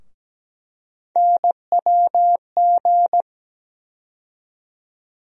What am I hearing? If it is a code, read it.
Morse code, heard as a keyed tone, about 17 wpm, 700 Hz. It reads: NWG
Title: USA Call Sign Postfixes - 3 Letters 17wpm